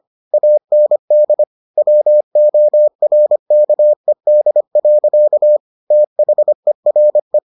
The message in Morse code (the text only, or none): AND WORKED. THERE